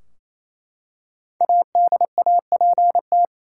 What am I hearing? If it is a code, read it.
Morse code, heard as a keyed tone, about 28 wpm, 700 Hz. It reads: ADAPT
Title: Five Letter Words - 28wpm